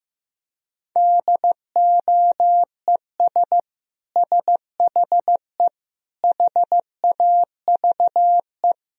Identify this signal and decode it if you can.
Morse code, heard as a keyed tone, about 15 wpm, 700 Hz. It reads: DOES SHE HAVE
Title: Sentences from Top 1000 Words - Review Part 2 - 15wpm